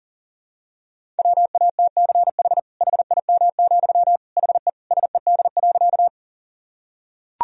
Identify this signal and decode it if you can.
Morse code, heard as a keyed tone, about 40 wpm, 700 Hz. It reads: WATCH HIM, HE SED.
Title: Sentences from Top 300 Words and Abbreviations 40wpm